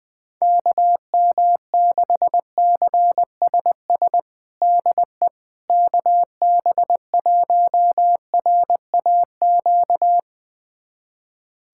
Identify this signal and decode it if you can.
Morse code, heard as a keyed tone, about 20 wpm, 700 Hz. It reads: KM6CSS DE KB1RAQ
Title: USA Callsign DE Callsign 20wpm